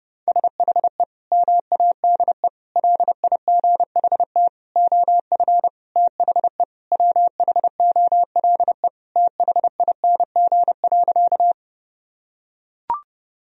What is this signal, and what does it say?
Morse code, heard as a keyed tone, about 32 wpm, 700 Hz. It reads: SHE MADE LIGHT OF THE WHOLE THING.
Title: Sentences from Top 1000 Words - Review Part 1 - 30wpm